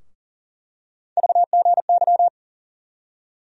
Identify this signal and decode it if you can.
Morse code, heard as a keyed tone, about 40 wpm, 700 Hz. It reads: VGY
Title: USA Call Sign Postfixes - 3 Letters 40wpm